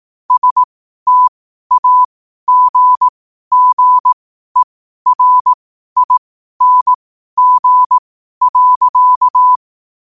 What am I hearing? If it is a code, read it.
Morse code, heard as a keyed tone, about 18 wpm, spaced as for 12 wpm, 1000 Hz. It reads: STAGGERING.
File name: qotdfiles / 12wpm